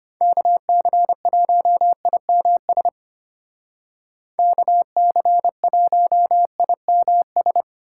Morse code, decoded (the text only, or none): KC1IMS KC1IMS